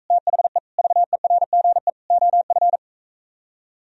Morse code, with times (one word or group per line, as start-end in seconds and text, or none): 0.1-0.6 THE
0.8-1.9 VERGE
2.1-2.8 OF